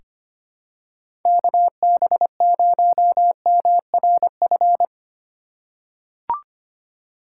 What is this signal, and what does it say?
Morse code, heard as a keyed tone, about 25 wpm, 700 Hz. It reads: KB0MRF
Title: USA Call Signs - 6 Characters - 2x Character Spacing - 25wpm